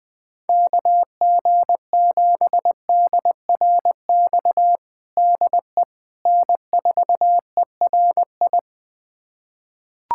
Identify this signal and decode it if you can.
Morse code, heard as a keyed tone, about 20 wpm, 700 Hz. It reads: KG7DRX DE N4ERI
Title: USA Callsign DE Callsign 20wpm